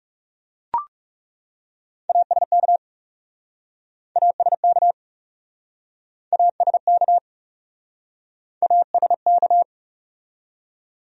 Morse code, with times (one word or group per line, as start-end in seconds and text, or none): 2.1-2.8 ASK
4.2-4.9 ASK
6.3-7.2 ASK
8.6-9.6 ASK